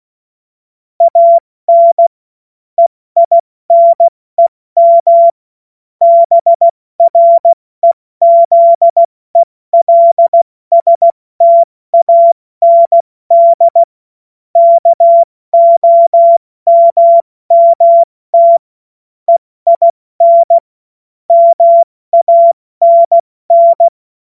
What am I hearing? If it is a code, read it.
Morse code, heard as a keyed tone, about 16 wpm, spaced as for 14 wpm, 680 Hz. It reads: AN EINEM BREZELSTAND KOMMT EIN MANN
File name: Morsekurs / Witze / Witz 15. November 2022 70 F80